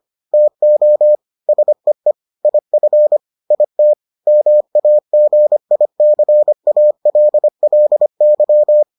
Morse code, heard as a keyed tone, about 25 wpm, 600 Hz. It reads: TO SEE IF IT MAGICALLY